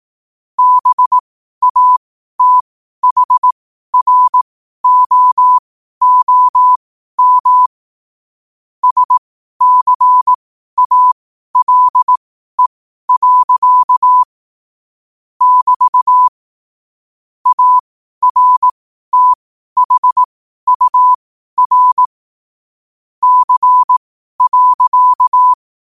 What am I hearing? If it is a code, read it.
Morse code, heard as a keyed tone, about 18 wpm, spaced as for 12 wpm, 1000 Hz. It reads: BATHROOM SCALE. <BT> ARTHUR C.
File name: qotdfiles / 12wpm